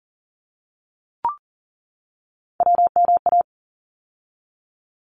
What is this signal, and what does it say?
Morse code, heard as a keyed tone, about 42 wpm, 700 Hz. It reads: WMA
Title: ARRL Sections 40wpm